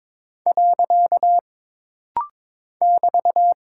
Morse code, heard as a keyed tone, about 22 wpm, 700 Hz. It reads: . <BT>
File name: practice / 22wpm / Single Punctuation-Prosign - Rapid-Fire - 22wpm